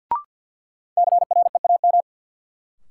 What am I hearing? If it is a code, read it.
Morse code, heard as a keyed tone, about 52 wpm, 700 Hz. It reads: CREAM
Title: Sentences from Top 1500 Words - New Word Drill - 50wpm